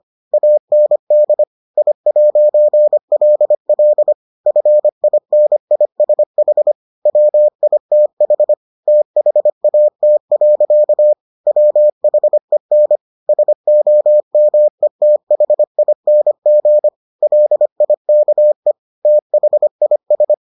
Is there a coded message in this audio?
Yes: Morse code from 0.3 to 20.3 s, heard as a keyed tone, about 25 wpm, 600 Hz. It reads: AND I'LL FINISH WITH THAT. WHEN SOMETHING LIKE THIS